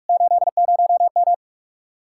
Morse code, heard as a keyed tone, about 56 wpm, 700 Hz. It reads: 80M
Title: Speed-Racing Common QSO Phrases - 45wpm